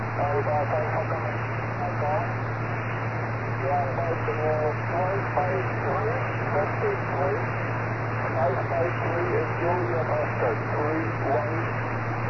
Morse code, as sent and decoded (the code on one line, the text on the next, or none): none